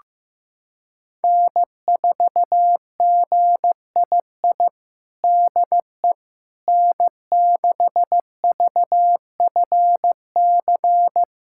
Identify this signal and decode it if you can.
Morse code, heard as a keyed tone, about 15 wpm, 700 Hz. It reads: N4GII DE N6VFC
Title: USA Callsign DE Callsign - 5 Characters - 15wpm